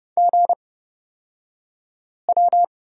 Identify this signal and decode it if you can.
Morse code, heard as a keyed tone, about 30 wpm, spaced as for 5 wpm, 700 Hz. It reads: GW
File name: Lesson 36d - BT - Two Characters 30wpm